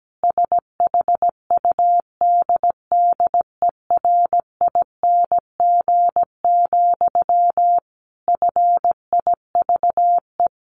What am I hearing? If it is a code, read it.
Morse code, heard as a keyed tone, about 17 wpm, 700 Hz. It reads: SHUDDERING, FIVE
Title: A Sound Of Thunder 17wpm